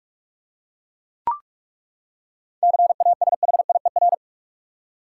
Morse code, heard as a keyed tone, about 42 wpm, 700 Hz. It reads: CASHIER
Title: Practice QSO - Exchange 3 - More Info - 45wpm